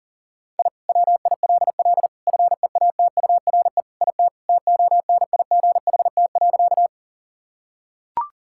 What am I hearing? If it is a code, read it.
Morse code, heard as a keyed tone, about 40 wpm, 700 Hz. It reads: I WILL FEATURE IT TONIGHT.
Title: Sentences from Top 1000 Words - Review Part 5 - 40wpm